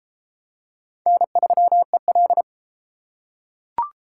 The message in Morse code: N3EL

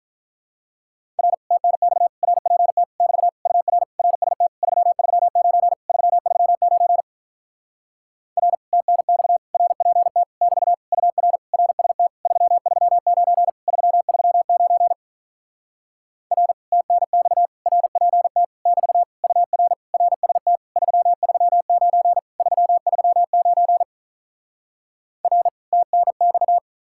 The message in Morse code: R TNX RPT <BT> UR RST 339 339 R TNX RPT <BT> UR RST 339 339 R TNX RPT <BT> UR RST 339 339 R TNX

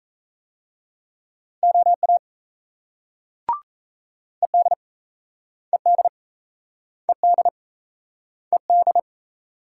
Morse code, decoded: OA ED ED ED ED

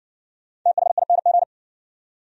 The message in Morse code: THING

